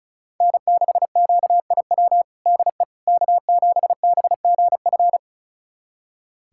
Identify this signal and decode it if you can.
Morse code, heard as a keyed tone, about 36 wpm, 700 Hz. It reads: N6QIW DE K7BGF
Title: USA Callsign DE Callsign - 5 Characters - 35wpm